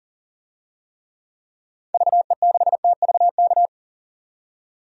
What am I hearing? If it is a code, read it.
Morse code, heard as a keyed tone, about 40 wpm, 700 Hz. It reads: VE6TVK
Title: DXCC-100 Call Signs - 6 Characters - 40wpm